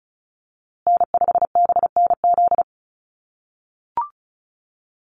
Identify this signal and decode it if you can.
Morse code, heard as a keyed tone, about 34 wpm, 700 Hz. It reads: N5BNZ
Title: USA Call Signs Repeated 3x 35wpm